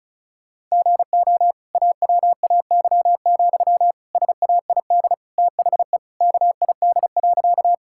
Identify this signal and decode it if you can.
Morse code, heard as a keyed tone, about 36 wpm, 700 Hz. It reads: GO AWAY, SAID THE KID.
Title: Sentences from Top 1600 Words 35wpm